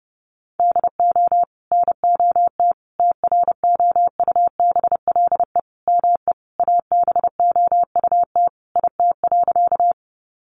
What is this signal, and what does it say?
Morse code, heard as a keyed tone, about 30 wpm, 700 Hz. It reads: DO NOT TROUBLE ME ABOUT IT.
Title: Sentences from Top 1200 Words 30wpm